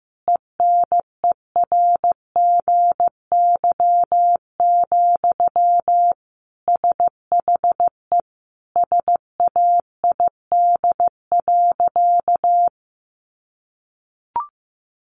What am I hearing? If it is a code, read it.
Morse code, heard as a keyed tone, about 15 wpm, 700 Hz. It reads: ENERGY, SHE SAID.
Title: Sentences from Top 900 Words 15wpm